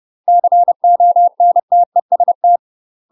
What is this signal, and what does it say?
Morse code, heard as a keyed tone, about 30 wpm, 700 Hz. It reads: CONTEST
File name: UK MASTER QSOs TRAINING / 30WPM / UK MASTER QSOs 0025